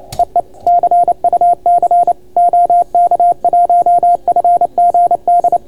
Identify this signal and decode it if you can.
Morse code, heard as a keyed tone, about 28 wpm, 690 Hz. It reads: EE CUC OK1FGD